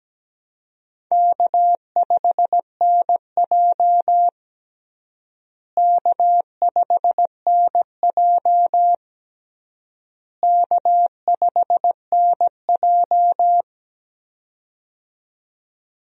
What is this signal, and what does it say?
Morse code, heard as a keyed tone, about 17 wpm, 700 Hz. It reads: K5NJ K5NJ K5NJ
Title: USA Call Signs - 4 Characters - Repeated 3x - 17wpm